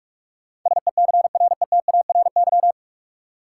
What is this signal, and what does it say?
Morse code, heard as a keyed tone, about 43 wpm, 700 Hz. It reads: SECRETARY
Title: Sentences from Top 1300 Words - New Word Drill - 45wpm